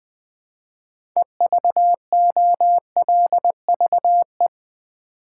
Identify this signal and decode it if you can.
Morse code, heard as a keyed tone, about 20 wpm, 700 Hz. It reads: EVOLVE
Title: 6 Letter Words - First 3 Letters Given - 20wpm